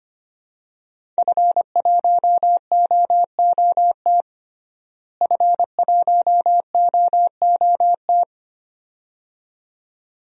Morse code, heard as a keyed tone, about 25 wpm, 700 Hz. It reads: F1OOT F1OOT
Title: DXCC-100 Call Signs - 5 Characters - Repeated 2x - 25wpm